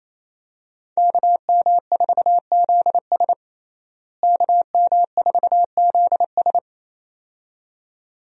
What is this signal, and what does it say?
Morse code, heard as a keyed tone, about 28 wpm, 700 Hz. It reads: KM4ZS KM4ZS
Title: USA Call Signs - 5 Characters - Repeated 2x - 28wpm